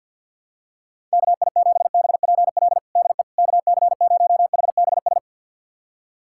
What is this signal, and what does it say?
Morse code, heard as a keyed tone, about 48 wpm, 700 Hz. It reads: KI7BPL DE KC0HBS
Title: USA Callsign DE Callsign - 6 Characters - 50wpm